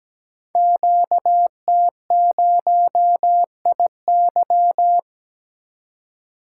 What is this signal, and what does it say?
Morse code, heard as a keyed tone, about 17 wpm, 700 Hz. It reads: QT0IY
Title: Code Groups - 5 Letter:Numbers - 17wpm